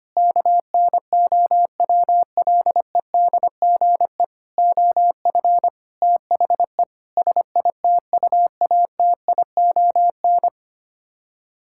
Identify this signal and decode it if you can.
Morse code, heard as a keyed tone, about 25 wpm, 700 Hz. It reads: KNOWLEDGE OF THE SITUATION
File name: Sets of 9 Words - 25wpm